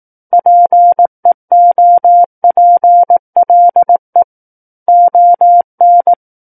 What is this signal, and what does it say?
Morse code, heard as a keyed tone, about 18 wpm, 700 Hz. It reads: PEOPLE ON
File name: Book One / 09-Chapter 9 - 18 WPM